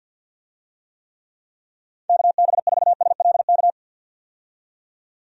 Morse code, heard as a keyed tone, about 50 wpm, 700 Hz. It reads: KB4SLK
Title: USA Call Signs - 6 Characters - 50wpm